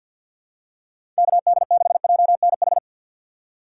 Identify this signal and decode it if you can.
Morse code, heard as a keyed tone, about 52 wpm, 700 Hz. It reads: KD6JNH
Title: USA Call Signs - 6 Characters - 3x Character Spacing - 50wpm